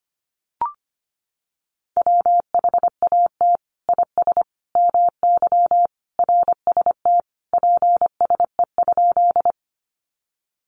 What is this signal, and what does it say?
Morse code, heard as a keyed tone, about 25 wpm, 700 Hz. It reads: WHAT IS MY RST PSE?
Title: Common QSO Phrases 25wpm